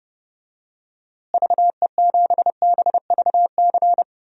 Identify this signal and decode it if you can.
Morse code, heard as a keyed tone, about 30 wpm, 700 Hz. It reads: VE7BVC
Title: DXCC-100 Call Signs - 6 Characters - Repeated 2x - 30wpm